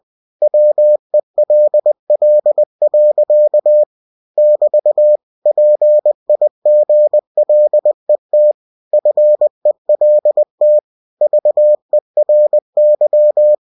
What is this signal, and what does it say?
Morse code, heard as a keyed tone, about 20 wpm, 600 Hz. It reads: WELL. <BT> PIGLET FELT VERY